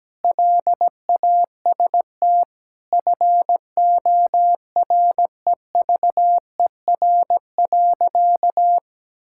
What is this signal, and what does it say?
Morse code, heard as a keyed tone, about 17 wpm, 700 Hz. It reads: LAST FOREVER.
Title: Sentences from Top 1600 Words 17wpm